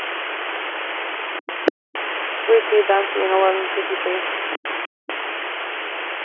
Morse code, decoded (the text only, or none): none